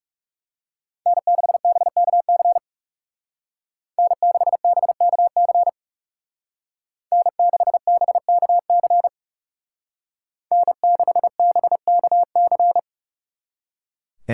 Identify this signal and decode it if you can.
Morse code, heard as a keyed tone, about 34 wpm, 700 Hz. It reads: N6BKC N6BKC N6BKC N6BKC
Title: Speed Racing - USA Call Signs 30wpm